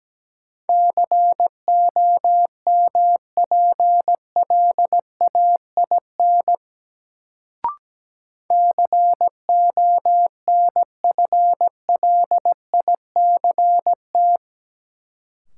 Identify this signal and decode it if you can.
Morse code, heard as a keyed tone, about 17 wpm, 700 Hz. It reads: COMPLAIN CONFLICT
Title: Sentences from Top 1300 Words - New Word Drill Alt - 17wpm